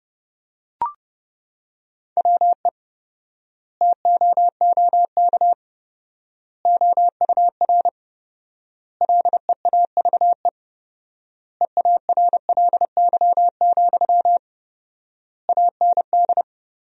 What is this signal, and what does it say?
Morse code, heard as a keyed tone, about 30 wpm, 700 Hz. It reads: WE TOOK OUR LEAVE EARLY, AND
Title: 4x Word-Spacing - Sentences from Top 400 Words - 30wpm